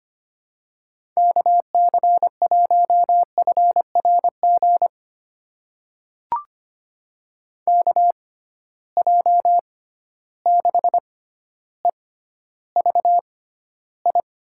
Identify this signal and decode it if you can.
Morse code, heard as a keyed tone, about 25 wpm, spaced as for 13 wpm, 700 Hz. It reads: KC1FRG KJ6EVI
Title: USA Call Signs - 6 Characters - 6x Character Spacing - 25wpm